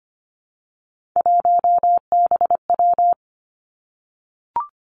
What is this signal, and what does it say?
Morse code, heard as a keyed tone, about 25 wpm, 700 Hz. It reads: 1BW
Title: Three Challenging Letter-Number Combinations 25wpm